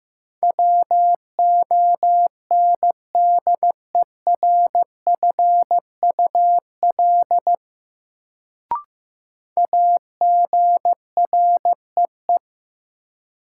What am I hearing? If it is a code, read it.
Morse code, heard as a keyed tone, about 15 wpm, 700 Hz. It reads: WONDERFUL AGREE
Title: Sentences from Top 1100 Words - New Word Drill Alt - 15wpm